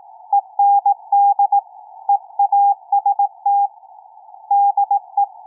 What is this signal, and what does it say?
Morse code, heard as a keyed tone, about 18 wpm, 800 Hz. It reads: END EAST DE
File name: k5cm-20m-9-11-by-ve2iq